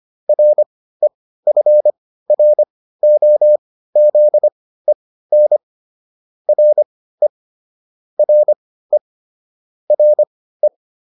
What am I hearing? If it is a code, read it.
Morse code, heard as a keyed tone, about 25 wpm, spaced as for 15 wpm, 600 Hz. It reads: REFROZEN RE RE RE